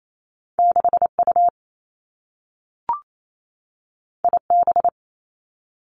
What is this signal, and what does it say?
Morse code, heard as a keyed tone, about 28 wpm, 700 Hz. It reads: 6 U I B
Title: Two Challenging Letter-Number Combinations 28wpm